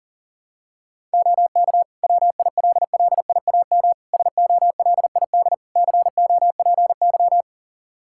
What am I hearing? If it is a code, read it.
Morse code, heard as a keyed tone, about 40 wpm, 700 Hz. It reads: OK WILLIAM SOLID COPY
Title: Practice QSO - Exchange 3 - More Info - 40wpm